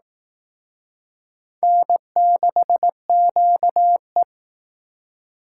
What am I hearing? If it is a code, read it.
Morse code, heard as a keyed tone, about 18 wpm, 700 Hz. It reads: N6QE
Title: Speed Racing - USA Call Signs 15wpm